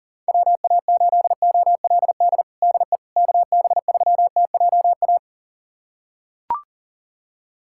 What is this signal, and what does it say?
Morse code, heard as a keyed tone, about 37 wpm, 700 Hz. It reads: WA8OLD DE KB3TJA
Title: USA Callsign DE Callsign - 6 Characters - 40wpm